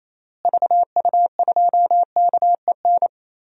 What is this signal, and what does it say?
Morse code, heard as a keyed tone, about 28 wpm, 700 Hz. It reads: VU2KEN
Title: DXCC-100 Call Signs - 6 Characters - Repeated 3x - 28wpm